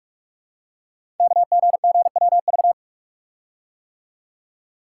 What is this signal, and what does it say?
Morse code, heard as a keyed tone, about 44 wpm, 700 Hz. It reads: KGGWV
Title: Code Groups - 5 Letters - 45wpm